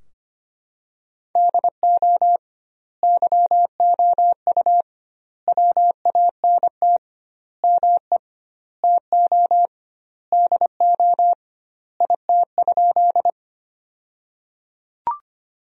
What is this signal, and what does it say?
Morse code, heard as a keyed tone, about 25 wpm, 700 Hz. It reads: DO YOU WANT ME TO DO IT?
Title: 2x Word-Spacing - Sentences from Top 200 Words - 25wpm